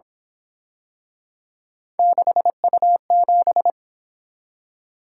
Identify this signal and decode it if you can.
Morse code, heard as a keyed tone, about 25 wpm, 700 Hz. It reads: 6U7
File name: Speed-Racing - Three Letter-Number Combinations - 22wpm